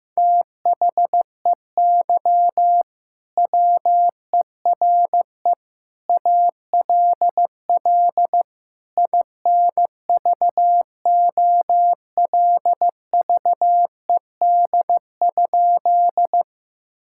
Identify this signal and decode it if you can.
Morse code, heard as a keyed tone, about 14 wpm, 700 Hz. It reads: THEY WERE ALL INVOLVED?
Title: Sentences from Top 1000 Words - Review Part 5 - 15wpm